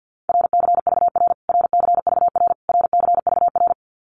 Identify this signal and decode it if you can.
Morse code, heard as a keyed tone, about 50 wpm, 700 Hz. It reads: RCVR RCVR RCVR
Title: Abbreviations Repeated 3x - 50wpm